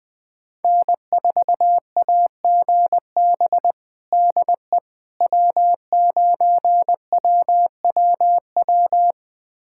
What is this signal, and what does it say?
Morse code, heard as a keyed tone, about 20 wpm, 700 Hz. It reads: N4AGB DE W9WWW
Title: USA Callsign DE Callsign - 5 Characters - 20wpm